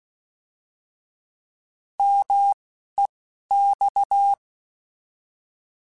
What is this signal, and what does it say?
Morse code, heard as a keyed tone, about 16 wpm, 800 Hz. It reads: M E X
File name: Contrasts / Contrasts-mp3s / 3LtrContrasts @10